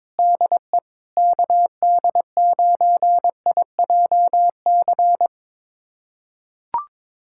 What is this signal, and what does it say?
Morse code, heard as a keyed tone, about 22 wpm, 700 Hz. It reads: DE KD9IJC
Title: USA Callsign DE Callsign - 6 Characters - 22wpm